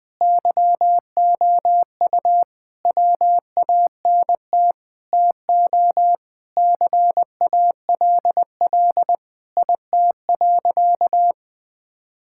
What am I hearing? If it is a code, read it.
Morse code, heard as a keyed tone, about 20 wpm, 700 Hz. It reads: YOU WANT TO CALL IT.
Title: Sentences with 7-Word Phrases - 20wpm